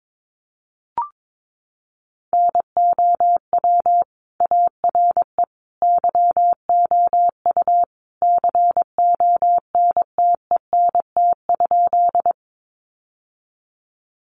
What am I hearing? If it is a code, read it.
Morse code, heard as a keyed tone, about 22 wpm, 700 Hz. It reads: NOW ARE YOU CONTENT?
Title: Sentences from Top 900 Words 22wpm